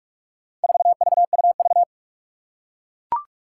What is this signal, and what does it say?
Morse code, heard as a keyed tone, about 41 wpm, 700 Hz. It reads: 4VUV
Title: Four Challenging Letter-Number Combinations 45wpm